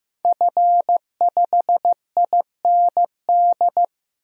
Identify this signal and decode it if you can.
Morse code, heard as a keyed tone, about 15 wpm, 700 Hz. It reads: F5IND
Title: DXCC-100 Call Signs - 5 Characters - Repeated 2x - 15wpm